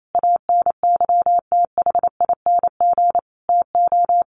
Morse code, heard as a keyed tone, about 28 wpm, 700 Hz. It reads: ANYTHING TO
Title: Sentences from Top 1300 Words 28wpm